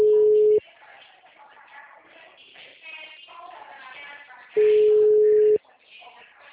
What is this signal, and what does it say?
Ringback tone (Europe, 425 Hz), with speech.